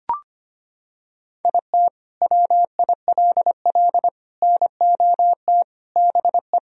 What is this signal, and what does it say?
Morse code, heard as a keyed tone, about 25 wpm, 700 Hz. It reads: IT WILL NOT BE
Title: Sentences from Top 1000 Words - Review Part 2 - 25wpm